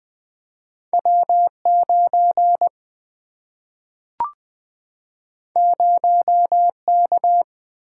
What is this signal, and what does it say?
Morse code, heard as a keyed tone, about 20 wpm, 700 Hz. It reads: W 9 0 K
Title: Single-Letter Single-Number 20wpm